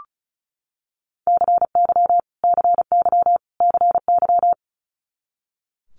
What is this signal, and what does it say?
Morse code, heard as a keyed tone, about 36 wpm, 700 Hz. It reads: CY CY CY
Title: Multi-letter Phonemes Repeated 3x 35wpm